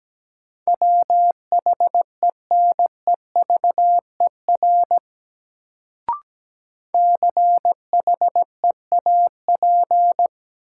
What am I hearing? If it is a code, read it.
Morse code, heard as a keyed tone, about 17 wpm, 700 Hz. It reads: WHENEVER CHEAP
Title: Sentences from Top 1200 Words - New Word Drill - 17wpm